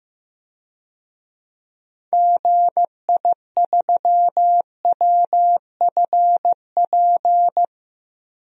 Morse code, heard as a keyed tone, about 15 wpm, 700 Hz. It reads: GI3WFP
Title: DXCC-100 Call Signs - 6 Characters - 15wpm